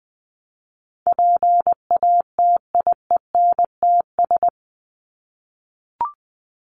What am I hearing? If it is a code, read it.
Morse code, heard as a keyed tone, about 20 wpm, 700 Hz. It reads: PATIENTS
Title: ES:S Suffix - Top 100 Words with Base Words 20wpm